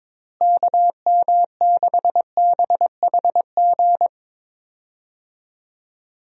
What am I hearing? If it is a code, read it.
Morse code, heard as a keyed tone, about 22 wpm, 700 Hz. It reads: KM6BHG
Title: USA Call Signs - 6 Characters - Repeated 3x - 22wpm